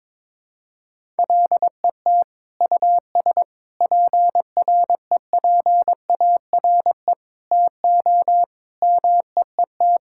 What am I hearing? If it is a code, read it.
Morse code, heard as a keyed tone, about 22 wpm, 700 Hz. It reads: LET US PREPARE TO MEET
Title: Sentences from Top 1000 Words - Review Part 6 - 22wpm